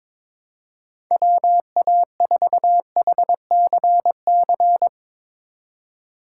Morse code, heard as a keyed tone, about 22 wpm, 700 Hz. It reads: WA4HCC
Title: K1USN SST QSO Practice - 22wpm